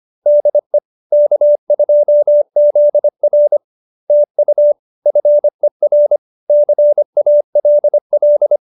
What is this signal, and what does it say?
Morse code, heard as a keyed tone, about 25 wpm, 590 Hz. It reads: DE K2ZR TU FER CALL